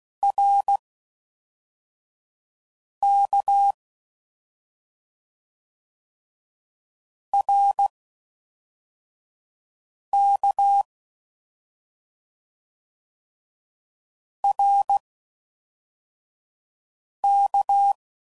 Morse code, heard as a keyed tone, about 16 wpm, spaced as for 4 wpm, 800 Hz. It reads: RK RK RK